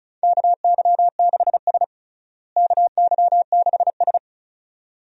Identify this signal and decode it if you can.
Morse code, heard as a keyed tone, about 36 wpm, 700 Hz. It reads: KY6S KY6S
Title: USA Call Signs - 4 Characters - Repeated 2x - 35wpm